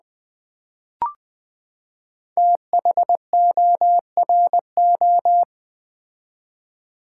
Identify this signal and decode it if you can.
Morse code, heard as a keyed tone, about 20 wpm, 700 Hz. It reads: THORO
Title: Build A Word - 5 Letters At A Time - 10-Letter Words - 20wpm